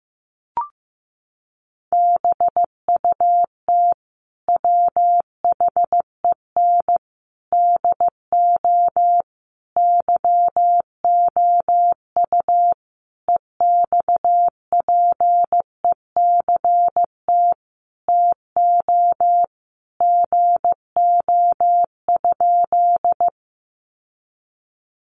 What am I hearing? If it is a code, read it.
Morse code, heard as a keyed tone, about 15 wpm, 700 Hz. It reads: BUT WHEN DO YOU EXPECT TO GO?
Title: Sentences from Top 1100 Words 15wpm